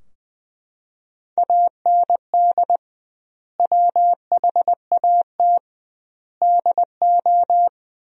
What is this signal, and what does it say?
Morse code, heard as a keyed tone, about 20 wpm, 700 Hz. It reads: AND WHAT DO
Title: 2x Word-Spacing - Sentences from Top 100 Words - 20wpm